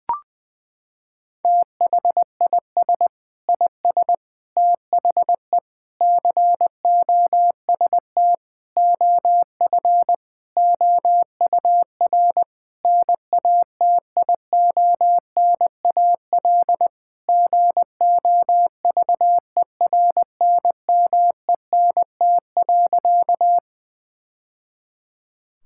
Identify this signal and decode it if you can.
Morse code, heard as a keyed tone, about 21 wpm, 700 Hz. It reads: THIS IS THE COST OF OUR NATIONAL GOVERNMENT.
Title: Sentences from Top 1000 Words - Review Part 5 - 20wpm